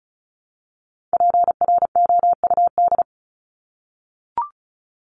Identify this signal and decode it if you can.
Morse code, heard as a keyed tone, about 34 wpm, 700 Hz. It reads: PROUD
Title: Top 1501-1750 Words 35wpm